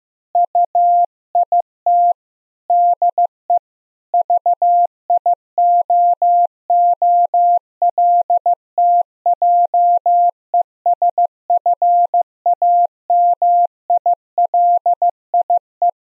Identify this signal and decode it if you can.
Morse code, heard as a keyed tone, about 12 wpm, 700 Hz. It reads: UIT DE VIOOLTJESFAMILIE
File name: Zendamateurnieuws en een stukje geseinde morse